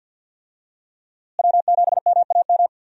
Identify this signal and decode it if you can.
Morse code, heard as a keyed tone, about 48 wpm, 700 Hz. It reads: W7MAM
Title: USA Call Signs - 5 Characters - Repeated 3x - 50wpm